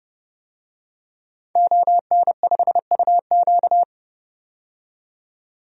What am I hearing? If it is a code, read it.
Morse code, heard as a keyed tone, about 30 wpm, 700 Hz. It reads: ON5UQ
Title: DXCC-100 Call Signs - 5 Characters - 30wpm